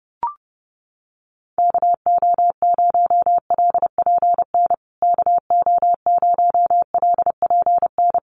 Morse code, heard as a keyed tone, about 30 wpm, 700 Hz. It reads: KO0LPN KO0LPN
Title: USA Call Signs Repeated 3x 30wpm